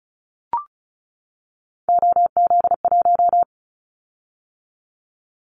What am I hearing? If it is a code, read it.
Morse code, heard as a keyed tone, about 36 wpm, 700 Hz. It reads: OZ1
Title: Callsign Prefixes for Top 100 DXCC - 35wpm